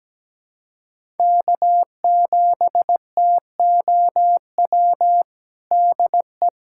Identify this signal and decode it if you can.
Morse code, heard as a keyed tone, about 17 wpm, 700 Hz. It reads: K7TOW DE
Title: USA Callsign DE Callsign - 5 Characters - 17wpm